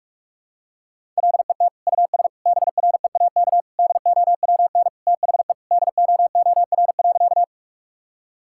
Morse code, heard as a keyed tone, about 46 wpm, 700 Hz. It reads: LET US BREAK DOWN THE DOOR.
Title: Sentences from Top 1000 Words - Review Part 6 - 45wpm